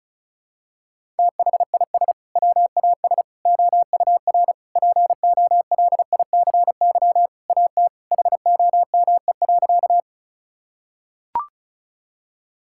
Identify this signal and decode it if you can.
Morse code, heard as a keyed tone, about 34 wpm, 700 Hz. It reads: THIS WAS OUR POLICY AT HOME.
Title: Sentences from Top 1000 Words - Review Part 5 - 35wpm